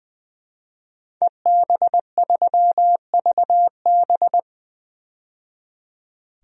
Morse code, heard as a keyed tone, about 20 wpm, 700 Hz. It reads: EB3VB
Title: DXCC-100 Call Signs - 5 Characters - 20wpm